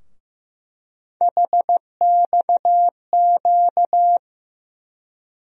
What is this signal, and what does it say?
Morse code, heard as a keyed tone, about 15 wpm, 700 Hz. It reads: HXQ
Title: USA Call Sign Postfixes - 3 Letters 15wpm